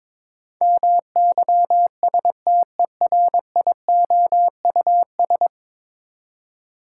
Morse code, heard as a keyed tone, about 22 wpm, 700 Hz. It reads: MYSTERIOUS
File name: practice / 22wpm / 10 Letter Words - First 3 Letters Given - 22wpm